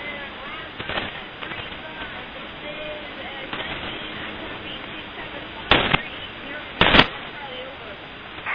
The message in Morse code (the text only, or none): none